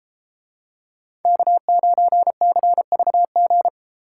Morse code, heard as a keyed tone, about 32 wpm, 700 Hz. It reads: K9CVG